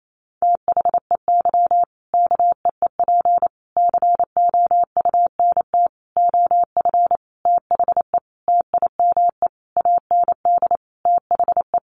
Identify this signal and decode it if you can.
Morse code, heard as a keyed tone, about 28 wpm, 700 Hz. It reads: THEY KEEP COUNT OF THE TIME AND THE
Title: Sentences from Top 1500 Words 28wpm